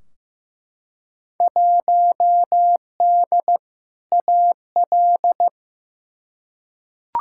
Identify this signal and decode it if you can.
Morse code, heard as a keyed tone, about 15 wpm, 700 Hz. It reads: 1D AL
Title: Field Day Exchanges - 15wpm